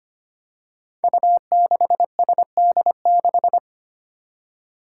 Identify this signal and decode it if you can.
Morse code, heard as a keyed tone, about 25 wpm, 700 Hz. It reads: U6SD6
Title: Five Challenging Letter-Number Combinations 25wpm